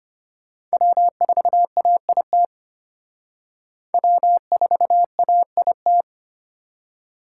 Morse code, heard as a keyed tone, about 32 wpm, 700 Hz. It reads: W4AIT W4AIT